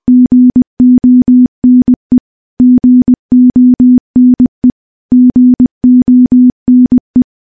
Morse code, heard as a keyed tone, about 20 wpm, 250 Hz. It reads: GONE GONE GONE
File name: morse-training-20WPM